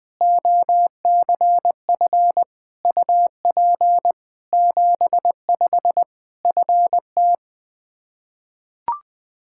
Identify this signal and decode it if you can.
Morse code, heard as a keyed tone, about 20 wpm, 700 Hz. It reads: OCF UP 75 FT
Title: Practice QSO - Exchange 2 - Info - 20wpm